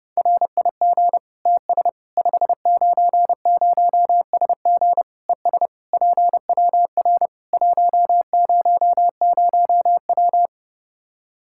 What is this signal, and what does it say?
Morse code, heard as a keyed tone, about 30 wpm, 700 Hz. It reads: RIG TS 590SG ES PWR 100W
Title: K1USN SST QSO Practice 30wpm